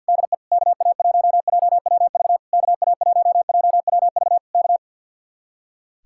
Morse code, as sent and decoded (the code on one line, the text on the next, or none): -.. / -.- .- .---- .--- .-- ...- / -.- .- .---- .--- .-- ...- / -.-
DE KA1JWV KA1JWV K